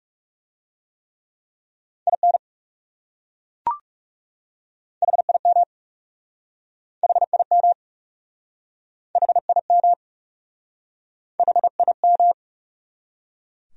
Morse code, heard as a keyed tone, about 37 wpm, 700 Hz. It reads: IN HIM HIM HIM HIM
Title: Speed-Racing Top 200 Words 30wpm